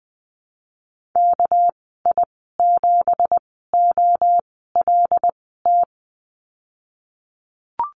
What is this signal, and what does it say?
Morse code, heard as a keyed tone, about 20 wpm, spaced as for 15 wpm, 700 Hz. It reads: KI7OLT
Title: 207 Lesson 33g - Letter j - Call Signs 20wpm